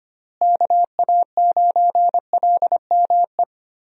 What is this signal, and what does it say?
Morse code, heard as a keyed tone, about 25 wpm, 700 Hz. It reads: KA9LME
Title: USA Call Signs - 6 Characters - Repeated 3x - 25wpm